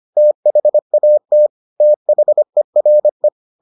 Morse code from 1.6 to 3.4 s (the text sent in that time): THERE